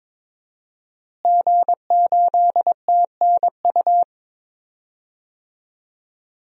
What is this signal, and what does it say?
Morse code, heard as a keyed tone, about 22 wpm, 700 Hz. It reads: G8TNU
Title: DXCC-100 Call Signs - 5 Characters - Repeated 3x - 22wpm